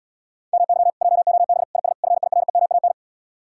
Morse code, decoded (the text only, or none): MY JOB IS FIREMAN